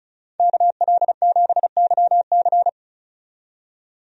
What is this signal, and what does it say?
Morse code, heard as a keyed tone, about 36 wpm, 700 Hz. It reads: KL7YC
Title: DXCC-100 Call Signs - 5 Characters - 35wpm